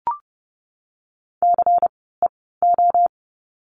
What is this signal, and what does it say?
Morse code, heard as a keyed tone, about 32 wpm, 700 Hz. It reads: C E O
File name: Lesson 16f - Letter c - Words 30wpm